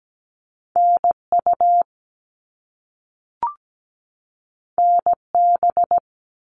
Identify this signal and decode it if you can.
Morse code, heard as a keyed tone, about 17 wpm, 700 Hz. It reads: NU NB
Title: Canadian Provinces and Territories 17wpm